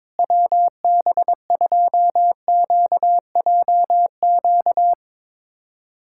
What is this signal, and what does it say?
Morse code, heard as a keyed tone, about 22 wpm, 700 Hz. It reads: WB2QJQ
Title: USA Callsign DE Callsign - 6 Characters - 22wpm